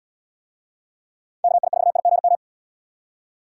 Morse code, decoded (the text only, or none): KI4IGM